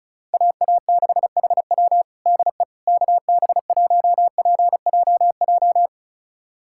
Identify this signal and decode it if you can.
Morse code, heard as a keyed tone, about 36 wpm, 700 Hz. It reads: AA6HW DE KB1PJJ
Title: USA Callsign DE Callsign 35wpm